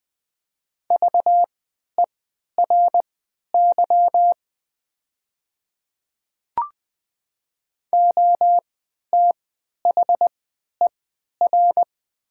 Morse code, heard as a keyed tone, about 20 wpm, spaced as for 11 wpm, 700 Hz. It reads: VERY OTHER
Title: 3x Character Spacing - Top 200 Words - 20wpm